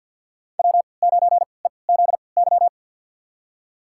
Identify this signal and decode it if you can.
Morse code, heard as a keyed tone, about 52 wpm, spaced as for 29 wpm, 700 Hz. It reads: W9EZY
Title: USA Call Signs - 5 Characters - 3x Character Spacing - 50wpm